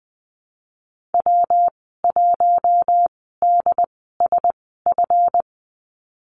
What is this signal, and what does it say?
Morse code, heard as a keyed tone, about 20 wpm, spaced as for 15 wpm, 700 Hz. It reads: W1DSF
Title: 147 Lesson 24g - Letter y - Call Signs 20wpm